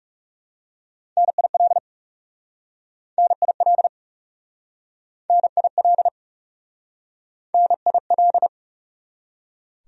Morse code, heard as a keyed tone, about 40 wpm, 700 Hz. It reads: NIL NIL NIL NIL